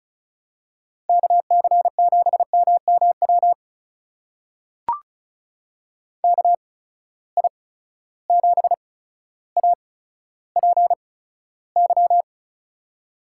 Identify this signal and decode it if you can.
Morse code, heard as a keyed tone, about 34 wpm, spaced as for 15 wpm, 700 Hz. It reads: KC7MMW KI7APY
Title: USA Call Signs - 6 Characters - 8x Character Spacing - 35wpm